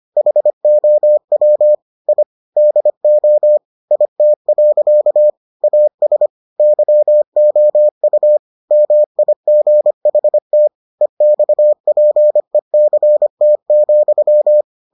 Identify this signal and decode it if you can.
Morse code, heard as a keyed tone, about 25 wpm, 600 Hz. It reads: HOW I DO IT. AS YOU MIGHT EXPECT,